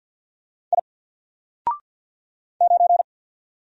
Morse code, heard as a keyed tone, about 48 wpm, 700 Hz. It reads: I 9
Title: Single Letter-Number 50wpm